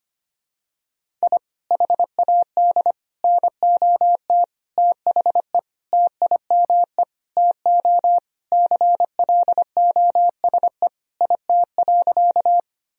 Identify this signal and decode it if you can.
Morse code, heard as a keyed tone, about 25 wpm, 700 Hz. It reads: I HAD NOT THE TIME TO CLOSE IT.
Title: Sentences from Top 1000 Words - Review Part 1 - 25wpm